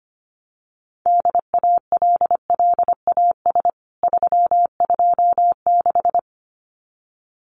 Morse code, heard as a keyed tone, about 25 wpm, 700 Hz. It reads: DALLAS 326
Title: CWT Exchange 25wpm